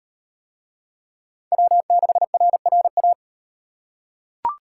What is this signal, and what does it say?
Morse code, heard as a keyed tone, about 38 wpm, 700 Hz. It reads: W6RRA